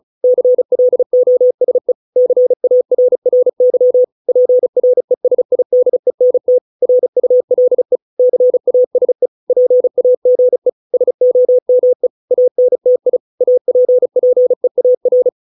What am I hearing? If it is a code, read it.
Morse code, heard as a keyed tone, about 34 wpm, 500 Hz. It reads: CLOSE CARRY PRESIDENT RULE CASE PAGE SOME ANTI APPEAR